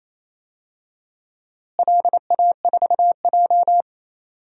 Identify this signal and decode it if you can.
Morse code, heard as a keyed tone, about 28 wpm, 700 Hz. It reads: LA4J
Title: DXCC-100 Call Signs - 4 Characters - Repeated 2x - 28wpm